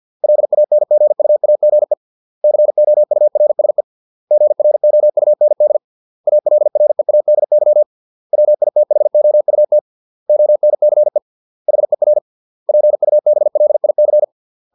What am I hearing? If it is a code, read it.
Morse code, heard as a keyed tone, about 50 wpm, 600 Hz. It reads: LANGUAGE COURSE GROUND ALREADY WITHOUT ONCE HER PUBLIC